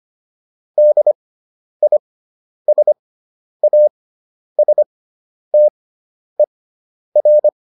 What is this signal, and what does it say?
Morse code, heard as a keyed tone, about 25 wpm, spaced as for 10 wpm, 600 Hz. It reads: DISASTER